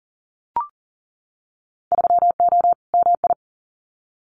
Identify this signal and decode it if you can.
Morse code, heard as a keyed tone, about 38 wpm, 700 Hz. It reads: 3O MI
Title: Winter Field Day Exchanges 40wpm